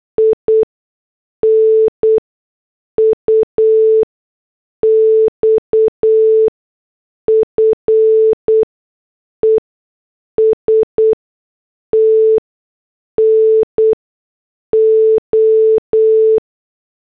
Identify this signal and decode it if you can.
Morse code, heard as a keyed tone, about 8 wpm, spaced as for 6 wpm, 440 Hz. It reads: INUXFEST NO